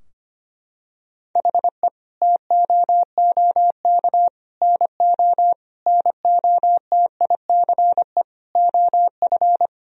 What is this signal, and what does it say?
Morse code, heard as a keyed tone, about 25 wpm, 700 Hz. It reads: HE TOOK NO NOTICE OF
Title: Sentences from Top 1000 Words - Review Part 2 - 25wpm